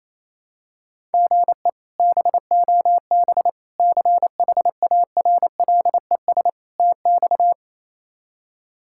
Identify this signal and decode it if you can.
Morse code, heard as a keyed tone, about 28 wpm, 700 Hz. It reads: GE BOB CHARLES TX
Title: K1USN SST QSO Practice - 28wpm